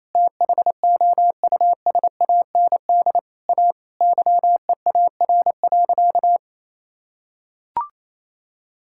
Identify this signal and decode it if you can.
Morse code, heard as a keyed tone, about 28 wpm, 700 Hz. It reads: THOUSAND A YEAR.
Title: Sentences from Top 1000 Words - Review Part 5 - 28wpm